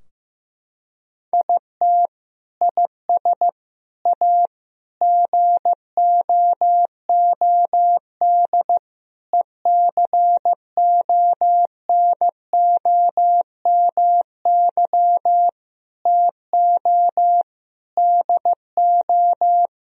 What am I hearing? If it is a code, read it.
Morse code, heard as a keyed tone, about 15 wpm, 700 Hz. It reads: IT IS A GOOD ECONOMY TO DO
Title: Sentences from Top 1000 Words - Review Part 3 - 15wpm